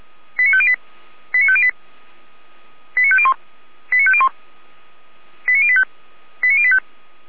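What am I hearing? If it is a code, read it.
Five-tone selective call (ZVEI), heard as quick runs of five tones, tones of about 70 ms. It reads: 89508 89508 89501 89501 89085 89085
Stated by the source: ZVEI five-tone selective call 89080, received by SDR.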